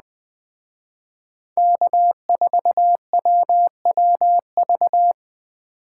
Morse code, heard as a keyed tone, about 20 wpm, 700 Hz. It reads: K4WWV
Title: Speed Racing - USA 1x3 Call Signs 17wpm